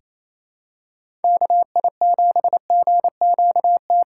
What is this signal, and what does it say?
Morse code, heard as a keyed tone, about 27 wpm, 700 Hz. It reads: KI7GQT
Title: USA Call Signs - 6 Characters - 4x Character Spacing - 28wpm